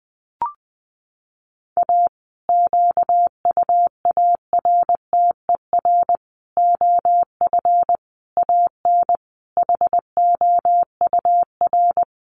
Morse code, heard as a keyed tone, about 20 wpm, 700 Hz. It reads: A QUARTER OF AN HOUR
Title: Sentences from Top 1300 Words 20wpm